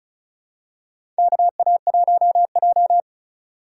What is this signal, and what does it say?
Morse code, heard as a keyed tone, about 36 wpm, 700 Hz. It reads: KA1J
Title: USA Call Signs - 4 Characters - 3x Character Spacing - 35wpm